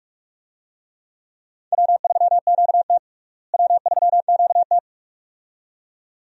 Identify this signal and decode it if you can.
Morse code, heard as a keyed tone, about 46 wpm, 700 Hz. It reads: W3QT W3QT
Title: USA Call Signs - 4 Characters - Repeated 2x - 45wpm